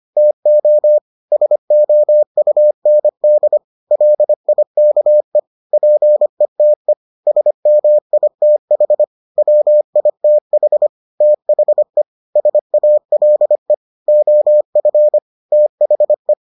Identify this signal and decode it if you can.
Morse code, heard as a keyed tone, about 25 wpm, 600 Hz. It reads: TO SOUND LIKE PETE SMITH WITH THE SALE OF THE